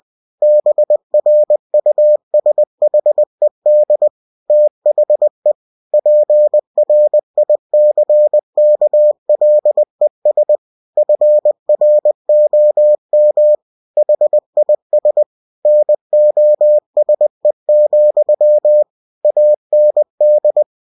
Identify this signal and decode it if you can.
Morse code, heard as a keyed tone, about 20 wpm, 600 Hz. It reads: BRUSHED THE PRICKLES FROM HIS NOSE, AND
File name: cw-books / winnie-the-pooh / winnie-20wpm-page-0004